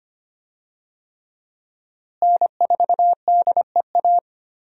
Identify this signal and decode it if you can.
Morse code, heard as a keyed tone, about 25 wpm, 700 Hz. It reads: N4DEA
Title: USA Call Signs - 5 Characters - Repeated 3x - 25wpm